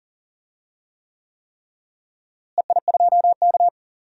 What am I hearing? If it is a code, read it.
Morse code, heard as a keyed tone, about 38 wpm, 700 Hz. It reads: EI2K
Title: DXCC-100 Call Signs - 4 Characters - Repeated 3x - 40wpm